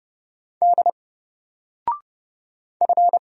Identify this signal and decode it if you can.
Morse code, heard as a keyed tone, about 31 wpm, 700 Hz. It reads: D F
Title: Single Letter-Number Alt 30wpm